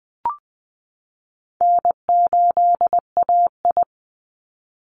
Morse code, heard as a keyed tone, about 20 wpm, 700 Hz. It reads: N8AI
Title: Call Signs Repeated 3x - 20wpm